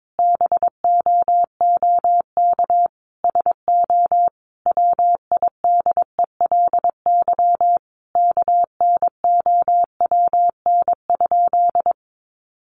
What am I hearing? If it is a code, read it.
Morse code, heard as a keyed tone, about 22 wpm, 700 Hz. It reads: BOOK SO WIDELY KNOWN?
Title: Sentences from Top 1200 Words 22wpm